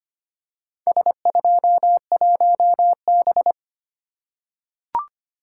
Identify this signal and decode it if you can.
Morse code, heard as a keyed tone, about 25 wpm, 700 Hz. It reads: S21B
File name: practice / 25wpm / Four Letter-Number Combinations - 25wpm